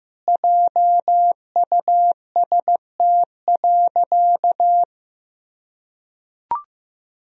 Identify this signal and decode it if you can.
Morse code, heard as a keyed tone, about 15 wpm, 700 Hz. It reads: JUST.
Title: Sentences from Top 1000 Words - Review Part 1 - 15wpm